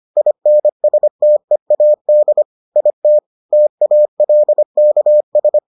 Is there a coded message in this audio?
Yes: Morse code from 0.2 to 5.6 s, heard as a keyed tone, about 25 wpm, 600 Hz. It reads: INSTEAD IT TALKS